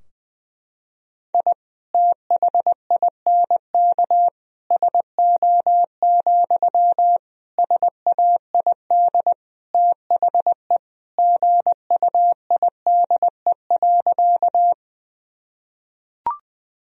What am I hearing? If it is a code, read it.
Morse code, heard as a keyed tone, about 20 wpm, 700 Hz. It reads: I THINK SO, SAID THE GUIDE.
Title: Sentences from Top 1000 Words - Review Part 3 - 20wpm